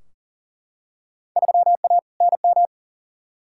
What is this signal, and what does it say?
Morse code, heard as a keyed tone, about 39 wpm, 700 Hz. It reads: 3A NM
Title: Field Day Exchanges Encore - 40wpm